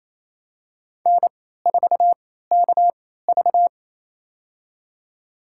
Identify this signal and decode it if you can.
Morse code, heard as a keyed tone, about 28 wpm, spaced as for 16 wpm, 700 Hz. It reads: N4KV
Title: USA Call Signs - 3x Character Spacing - 28wpm